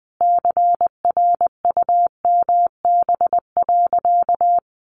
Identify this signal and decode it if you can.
Morse code, heard as a keyed tone, about 20 wpm, 700 Hz. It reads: CRUMB.
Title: Additional Easy Phrases 20wpm